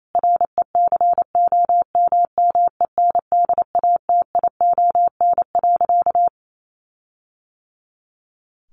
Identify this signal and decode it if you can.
Morse code, heard as a keyed tone, about 28 wpm, 700 Hz. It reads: RECOMMENDATION.
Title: Sentences from Top 1400 Words 28wpm